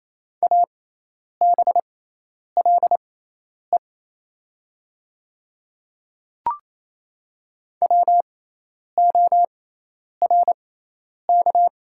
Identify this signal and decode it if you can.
Morse code, heard as a keyed tone, about 28 wpm, spaced as for 10 wpm, 700 Hz. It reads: ABLE WORK
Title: 6x Character Spacing - Top 500 Words - 28wpm